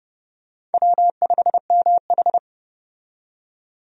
Morse code, heard as a keyed tone, about 31 wpm, 700 Hz. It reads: W5MH
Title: USA Call Signs - 4 Characters - 30wpm